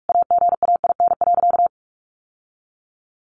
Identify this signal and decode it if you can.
Morse code, heard as a keyed tone, about 44 wpm, 700 Hz. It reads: AGAIN.
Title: Sentences from Top 1100 Words 45wpm